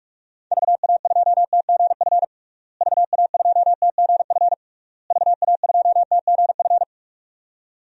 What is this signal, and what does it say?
Morse code, heard as a keyed tone, about 46 wpm, 700 Hz. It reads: VA2TGF VA2TGF VA2TGF
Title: DXCC-100 Call Signs - 6 Characters - Repeated 3x - 45wpm